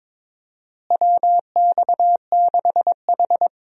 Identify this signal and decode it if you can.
Morse code, heard as a keyed tone, about 22 wpm, 700 Hz. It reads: WX6H
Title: USA Call Signs - 4 Characters - Repeated 2x - 22wpm